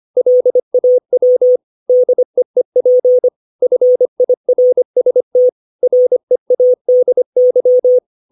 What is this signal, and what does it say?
Morse code, heard as a keyed tone, about 25 wpm, 500 Hz. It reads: LAW DEEP FIRST READY
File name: 632 words / 632 Words 25WPM